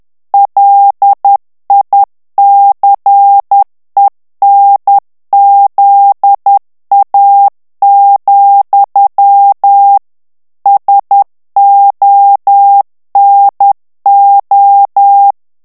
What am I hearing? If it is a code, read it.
Morse code, heard as a keyed tone, about 10 wpm, 800 Hz. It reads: LICENZA, SONO
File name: ITALIEN texte clair 2 vitesse 10 mots minute